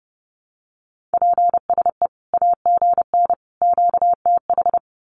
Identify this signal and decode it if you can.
Morse code, heard as a keyed tone, about 30 wpm, 700 Hz. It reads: PSE AGN QTH
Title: Common QSO Phrases 30wpm